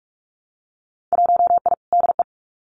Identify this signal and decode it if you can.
Morse code, heard as a keyed tone, about 46 wpm, 690 Hz. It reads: 1I DE
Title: Winter Field Day Exchanges Encore 45wpm